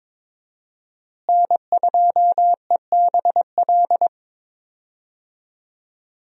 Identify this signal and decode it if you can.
Morse code, heard as a keyed tone, about 22 wpm, 700 Hz. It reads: N2EBL
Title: USA Call Signs - 5 Characters - Repeated 3x - 22wpm